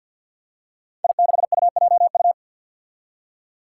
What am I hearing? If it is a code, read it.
Morse code, heard as a keyed tone, about 53 wpm, 700 Hz. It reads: I6UJU
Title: Five Challenging Letter-Number Combinations 50wpm